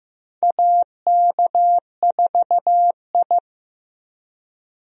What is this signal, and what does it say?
Morse code, heard as a keyed tone, about 15 wpm, 700 Hz. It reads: AK4I
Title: USA Call Signs - 4 Characters - 15wpm